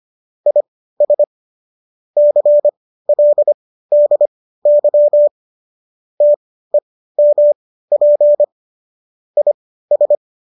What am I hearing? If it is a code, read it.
Morse code, heard as a keyed tone, about 25 wpm, spaced as for 15 wpm, 590 Hz. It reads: IS CLDY TEMP IS
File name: qso208 15